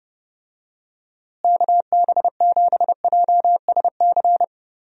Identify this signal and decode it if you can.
Morse code, heard as a keyed tone, about 30 wpm, 700 Hz. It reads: KB7JSC
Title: USA Call Signs - 6 Characters - Repeated 2x - 30wpm